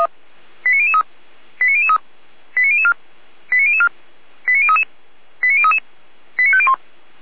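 Five-tone selective call (ZVEI), heard as quick runs of five tones, tones of about 80 ms. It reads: 89003 89003 89004 89004 89033 89033 89501
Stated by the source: ZVEI five-tone selective call received via SDR, from the Feuerwehr Orlamünde audio archive.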